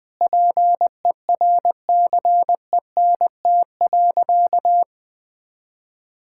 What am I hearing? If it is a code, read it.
Morse code, heard as a keyed tone, about 21 wpm, 700 Hz. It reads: PERCENT.
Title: Sentences with 6-Word Phrases - Spoken Beginning - 20wpm